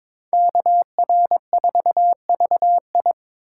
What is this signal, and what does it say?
Morse code, heard as a keyed tone, about 26 wpm, 700 Hz. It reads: KR4VI
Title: Speed Racing - USA Call Signs 22wpm